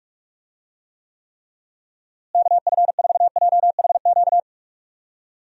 Morse code, heard as a keyed tone, about 43 wpm, 700 Hz. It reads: KF4JHQ
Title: USA Call Signs - 6 Characters - Repeated 3x - 45wpm